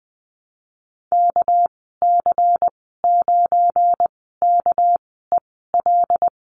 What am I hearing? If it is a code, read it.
Morse code, heard as a keyed tone, about 20 wpm, spaced as for 14 wpm, 700 Hz. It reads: KC9KEL
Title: 213 Lesson 34g - Number 8 - Call Signs 20wpm